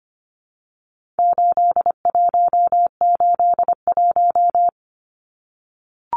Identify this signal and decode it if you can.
Morse code, heard as a keyed tone, about 25 wpm, 700 Hz. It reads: 8181
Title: Numbers - 4-Digit 25wpm